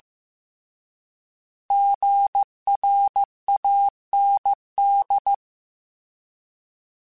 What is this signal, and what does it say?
Morse code, heard as a keyed tone, about 15 wpm, 800 Hz. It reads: GRAND